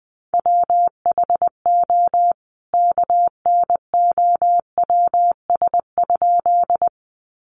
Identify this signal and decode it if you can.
Morse code, heard as a keyed tone, about 20 wpm, 700 Hz. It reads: WHO KNOWS?
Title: A Sound Of Thunder 20wpm